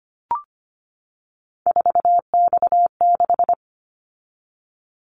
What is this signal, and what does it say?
Morse code, heard as a keyed tone, about 25 wpm, 700 Hz. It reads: 4X6
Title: Callsign Prefixes for Top 50 DXCC - 25wpm